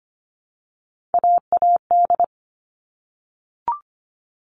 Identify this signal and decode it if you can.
Morse code, heard as a keyed tone, about 25 wpm, 700 Hz. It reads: AAD
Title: All Cut Numbers - 3-Digit 25wpm